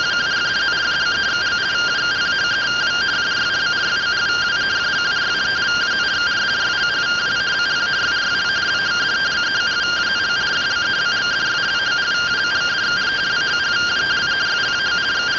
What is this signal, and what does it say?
Two-tone teleprinter signal, about 170 Hz shift.
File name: findings / 81 axial iodine 262.gen-23 rtty rx